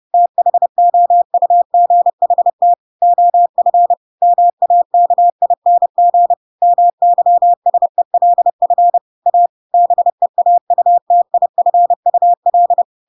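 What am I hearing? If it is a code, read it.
Morse code, heard as a keyed tone, about 30 wpm, 700 Hz. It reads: THOUGHT OF MAKING MYSELF A BEAUTIFUL
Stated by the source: Chapter 2 of The Adventures of Pinocchio read as a book in Morse code (CW) at 30 wpm, by ik1hge.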